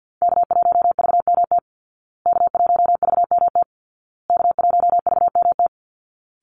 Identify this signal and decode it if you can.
Morse code, heard as a keyed tone, about 48 wpm, 700 Hz. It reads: K1VMT K1VMT K1VMT
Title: Call Signs Repeated 2x - 50wpm